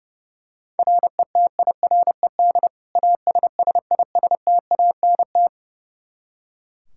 Morse code, heard as a keyed tone, about 30 wpm, 700 Hz. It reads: RETIRED ASSISTANT
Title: Practice QSO - Exchange 3 - More Info - 30wpm